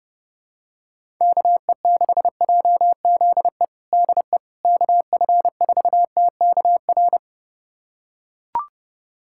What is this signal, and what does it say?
Morse code, heard as a keyed tone, about 32 wpm, 700 Hz. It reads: KE6JZE DE KF4TKR
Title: USA Callsign DE Callsign - 6 Characters - 30wpm